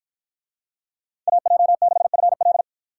Morse code, heard as a keyed tone, about 52 wpm, 700 Hz. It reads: AJ6FL